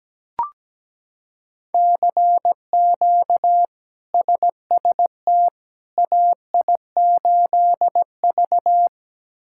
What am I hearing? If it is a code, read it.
Morse code, heard as a keyed tone, about 17 wpm, 700 Hz. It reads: CQ SST AI8V
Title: K1USN SST QSO Practice - 17wpm